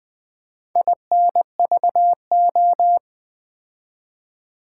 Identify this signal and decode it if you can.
Morse code, heard as a keyed tone, about 20 wpm, 700 Hz. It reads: INVO
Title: Build A Word - 2 Letters At A Time - 6-Letter Words - 20wpm